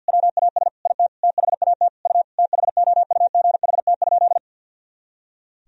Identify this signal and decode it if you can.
Morse code, heard as a keyed tone, about 46 wpm, 700 Hz. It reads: WAS IT THAT U THOUGHT?
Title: Sentences from Top 300 Words and Abbreviations 50wpm